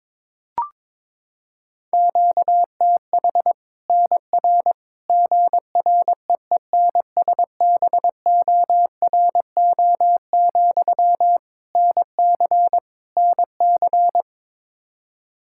Morse code, heard as a keyed tone, about 22 wpm, 700 Hz. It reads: QTH NR GREENSBORO, NC NC
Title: Practice QSO - Exchange 1 - Beginning - 22wpm